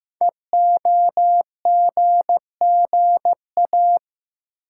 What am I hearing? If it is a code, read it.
Morse code, heard as a keyed tone, about 15 wpm, 700 Hz. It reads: EOGGA
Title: Code Groups - 5 Letters - 15wpm